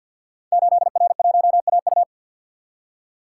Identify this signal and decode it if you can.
Morse code, heard as a keyed tone, about 48 wpm, 700 Hz. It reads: 8R1AU
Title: Code Groups - 5 Letter:Numbers - 50wpm